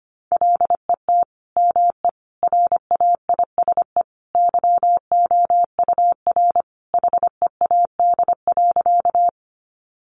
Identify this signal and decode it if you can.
Morse code, heard as a keyed tone, about 25 wpm, 700 Hz. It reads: LET ME RAISE YOUR HEAD.
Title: Sentences from Top 1400 Words 25wpm